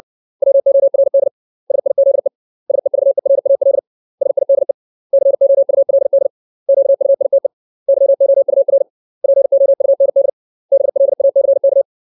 Morse code, heard as a keyed tone, about 60 wpm, 550 Hz. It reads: WORD SIZE SEVERAL SIDE COULD QUITE YOUR POUND BLACK
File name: 632 words / 632 Words 60WPM